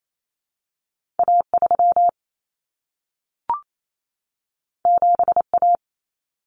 Morse code, heard as a keyed tone, about 28 wpm, 700 Hz. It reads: A3 7A
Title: Common Cut Numbers - 2-Digit 28wpm